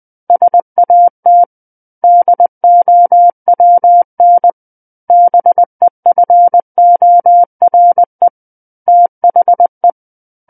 Morse code, heard as a keyed tone, about 20 wpm, 700 Hz. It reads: SAT DOWN BEFORE THE
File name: 20 WPM